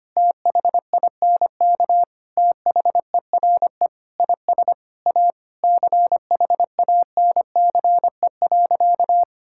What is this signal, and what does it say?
Morse code, heard as a keyed tone, about 25 wpm, 700 Hz. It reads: THINK THERE IS A CHANCE.
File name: practice / 25wpm / Sentences with 4-Word Phrases - 25wpm